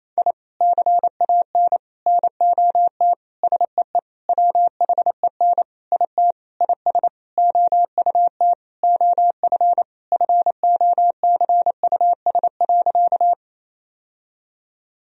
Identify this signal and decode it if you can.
Morse code, heard as a keyed tone, about 27 wpm, 700 Hz. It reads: I CAN NOT SEE WHEN IT IS OUT OF FOCUS.
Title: Sentences from Top 1000 Words - Review Part 4 - 28wpm